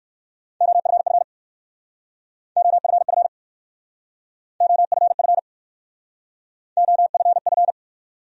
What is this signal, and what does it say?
Morse code, heard as a keyed tone, about 58 wpm, 700 Hz. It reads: OFF OFF OFF OFF